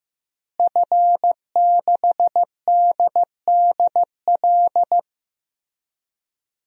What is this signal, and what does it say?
Morse code, heard as a keyed tone, about 15 wpm, 700 Hz. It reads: F6DDL
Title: DXCC-100 Call Signs - 5 Characters - Repeated 3x - 15wpm